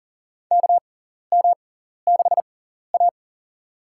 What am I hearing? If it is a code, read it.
Morse code, heard as a keyed tone, about 40 wpm, spaced as for 14 wpm, 700 Hz. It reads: KM6A
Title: USA Call Signs - 4 Characters - 6x Character Spacing - 40wpm